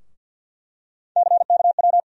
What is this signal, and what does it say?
Morse code, heard as a keyed tone, about 52 wpm, 700 Hz. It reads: CKW
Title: USA Call Sign Postfixes - 3 Letters 50wpm